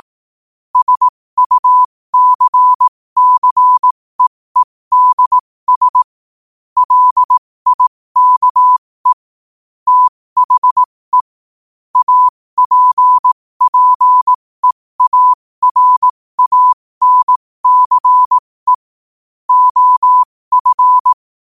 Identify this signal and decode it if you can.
Morse code, heard as a keyed tone, about 18 wpm, spaced as for 15 wpm, 1000 Hz. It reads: SUCCEEDS LIKE THE APPEARANCE OF
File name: qotdfiles / 15wpm